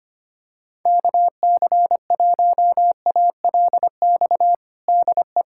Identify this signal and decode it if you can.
Morse code, heard as a keyed tone, about 25 wpm, 700 Hz. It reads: KC1ALX DE
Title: USA Callsign DE Callsign - 6 Characters - 25wpm